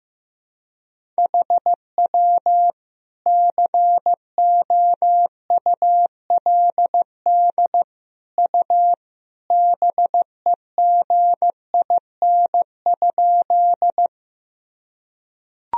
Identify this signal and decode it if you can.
Morse code, heard as a keyed tone, about 15 wpm, 700 Hz. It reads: HW COULD U BEGIN?
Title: Sentences from Top 300 Words and Abbreviations 15wpm